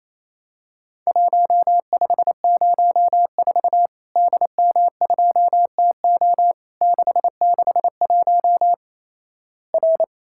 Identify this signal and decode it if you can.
Morse code, heard as a keyed tone, about 28 wpm, 700 Hz. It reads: 1504 DM2TO 661 R
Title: Worked All Europe DX Contest - QTC Exchange - 28wpm